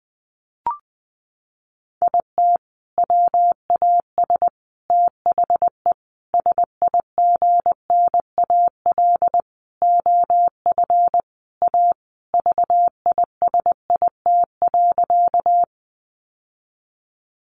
Morse code, heard as a keyed tone, about 19 wpm, 700 Hz. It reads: IT WAS THE SIGNAL OF A VISIT.
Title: Sentences from Top 1400 Words 20wpm